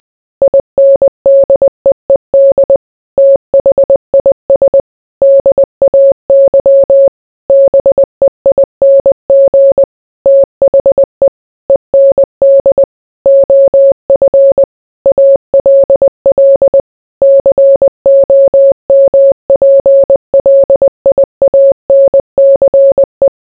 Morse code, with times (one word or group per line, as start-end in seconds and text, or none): 0.4-2.8 INDEED
3.2-4.8 THIS
5.2-7.1 DAY
7.5-9.8 BEING
10.3-11.3 THE
11.7-12.8 END
13.3-14.6 OF
15.1-16.8 ALL
17.2-23.3 COMPLIANCE